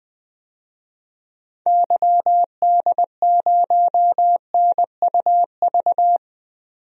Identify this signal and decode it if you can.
Morse code, heard as a keyed tone, about 20 wpm, 700 Hz. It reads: YD0NUV
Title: DXCC-100 Call Signs - 6 Characters - 20wpm